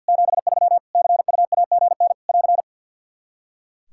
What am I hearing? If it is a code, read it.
Morse code, heard as a keyed tone, about 45 wpm, 700 Hz. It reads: 73 CUAGN <AR>
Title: Practice QSO - Exchange 4 - Ending - 50wpm